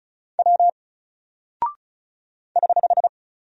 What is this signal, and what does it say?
Morse code, heard as a keyed tone, about 36 wpm, 700 Hz. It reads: W <HH>
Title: Single Character (Letter:Number:Punctuation:Prosign) Alt 35wpm